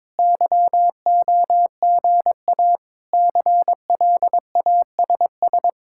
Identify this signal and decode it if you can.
Morse code, heard as a keyed tone, about 22 wpm, 700 Hz. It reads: YOGA CLASS
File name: Difficult Sentences with Hints - 22wpm